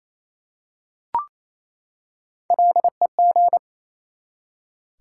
Morse code, heard as a keyed tone, about 28 wpm, 700 Hz. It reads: LEG
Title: Three Letter Words - 28wpm